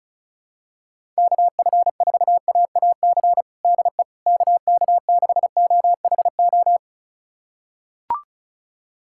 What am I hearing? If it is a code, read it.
Morse code, heard as a keyed tone, about 36 wpm, 700 Hz. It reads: KF4AAC DE KK6OHO
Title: USA Callsign DE Callsign 35wpm